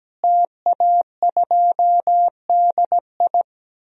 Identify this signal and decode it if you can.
Morse code, heard as a keyed tone, about 17 wpm, 700 Hz. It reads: TA2DI
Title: DXCC-100 Call Signs - 5 Characters - Repeated 2x - 17wpm